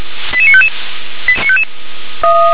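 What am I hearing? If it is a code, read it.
Five-tone selective call (ZVEI), heard as quick runs of five tones, tones of about 70 ms. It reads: 88955
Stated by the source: ZVEI five-tone call received on SDR.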